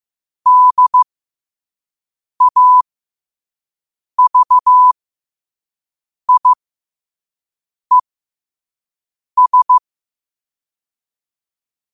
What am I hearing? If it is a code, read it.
Morse code, heard as a keyed tone, about 15 wpm, spaced as for 5 wpm, 1000 Hz. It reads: DAVIES
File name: qotdfiles / 5wpm